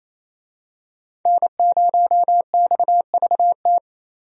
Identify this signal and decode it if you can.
Morse code, heard as a keyed tone, about 28 wpm, 700 Hz. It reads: N0XVT
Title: USA Call Signs - 5 Characters - Repeated 2x - 28wpm